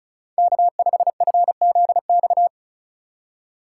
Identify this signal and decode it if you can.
Morse code, heard as a keyed tone, about 43 wpm, 700 Hz. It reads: K5FZX
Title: Speed Racing - USA 1x3 Call Signs 35wpm